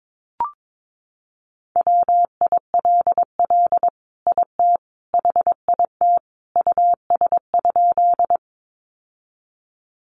Morse code, heard as a keyed tone, about 22 wpm, 700 Hz. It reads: WILL IT HIT US?
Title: Sentences from Top 1200 Words 22wpm